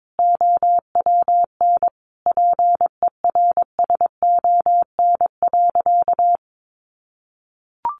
Morse code, heard as a keyed tone, about 22 wpm, 700 Hz. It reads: OWN PERSON.
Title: Sentences from Top 600 Words 22wpm